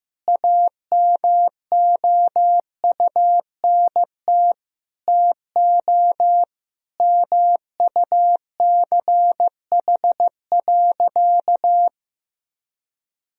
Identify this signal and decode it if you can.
Morse code, heard as a keyed tone, about 15 wpm, 700 Hz. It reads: AMOUNT TO MUCH.
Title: Sentences from Top 1000 Words - Review Part 4 - 15wpm